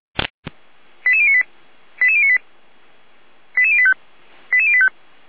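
Five-tone selective call (ZVEI), heard as quick runs of five tones, tones of about 70 ms. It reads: 88098 88098 88085 88085
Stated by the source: Five-tone ZVEI selective call received on SDR.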